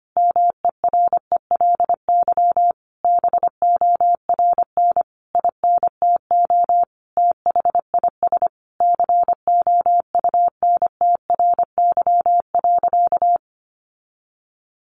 Morse code, heard as a keyed tone, about 25 wpm, 700 Hz. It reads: MERELY BORN INTO THIS COUNTRY.
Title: Sentences from Top 1400 Words 25wpm